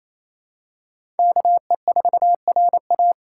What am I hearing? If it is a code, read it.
Morse code, heard as a keyed tone, about 27 wpm, 700 Hz. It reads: KE4RA
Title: USA Call Signs - 4x Character Spacing - 28wpm